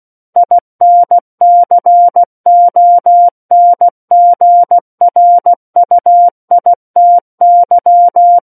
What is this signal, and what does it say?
Morse code, heard as a keyed tone, about 16 wpm, 700 Hz. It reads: INCONGRUITY
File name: Book One / 07-Chapter 7 - 16 WPM